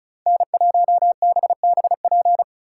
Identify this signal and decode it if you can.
Morse code, heard as a keyed tone, about 43 wpm, 700 Hz. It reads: N1BBP
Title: Speed Racing - USA 1x3 Call Signs 35wpm